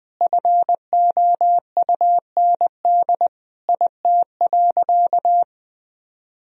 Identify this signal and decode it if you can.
Morse code, heard as a keyed tone, about 21 wpm, 700 Hz. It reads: FOUND IT.
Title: Sentences from Top 1000 Words - Review Part 1 - 20wpm